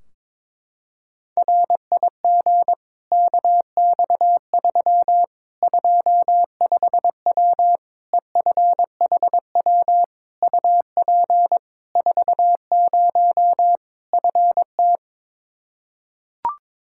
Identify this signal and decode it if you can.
Morse code, heard as a keyed tone, about 22 wpm, 700 Hz. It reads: RIG KX3 25W EFHW UP 40 FT
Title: Practice QSO - Exchange 2 - Info - 22wpm